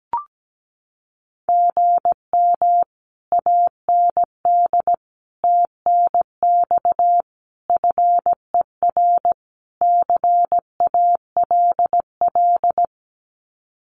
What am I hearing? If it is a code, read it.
Morse code, heard as a keyed tone, about 17 wpm, 700 Hz. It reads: GM AND TNX FER CALL
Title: Common QSO Phrases Encore 17wpm